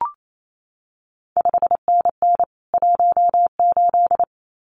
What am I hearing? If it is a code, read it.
Morse code, heard as a keyed tone, about 28 wpm, 700 Hz. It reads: 5NN 18
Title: CQ WW DX Contest Exchange Encore - 28wpm